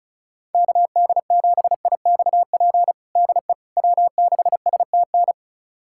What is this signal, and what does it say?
Morse code, heard as a keyed tone, about 36 wpm, 700 Hz. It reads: KD7IXP DE W6STN
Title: USA Callsign DE Callsign 35wpm